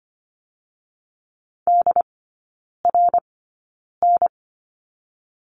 Morse code, heard as a keyed tone, about 25 wpm, spaced as for 9 wpm, 700 Hz. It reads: DRN
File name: Lesson 17e - Letter u - Three Characters 25wpm